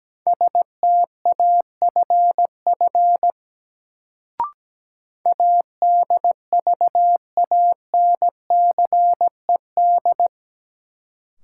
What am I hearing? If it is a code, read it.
Morse code, heard as a keyed tone, about 17 wpm, 700 Hz. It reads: STAFF ADVANCED
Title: Sentences from Top 1200 Words - New Word Drill - 17wpm